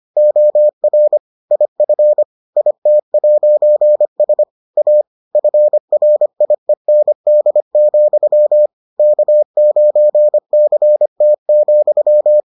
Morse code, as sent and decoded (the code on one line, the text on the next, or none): --- .-. / .. ..-. / .. - .----. ... / .- / ..-. .-. .. -. -.. --..-- / -.- ----. -.-. - --..--
OR IF IT'S A FRIEND, K9CT,